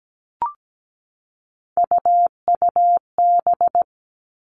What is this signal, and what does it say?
Morse code, heard as a keyed tone, about 17 wpm, 700 Hz. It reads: UUB
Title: Three Challenging Letter-Number Combinations 17wpm